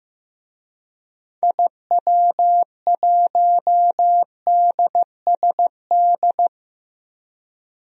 Morse code, heard as a keyed tone, about 15 wpm, 700 Hz. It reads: IW1DSD
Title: DXCC-100 Call Signs - 6 Characters - Repeated 2x - 15wpm